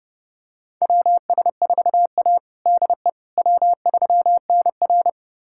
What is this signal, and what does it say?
Morse code, heard as a keyed tone, about 32 wpm, 700 Hz. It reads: WS4A DE W3NR
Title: USA Callsign DE Callsign - 4 Characters - 30wpm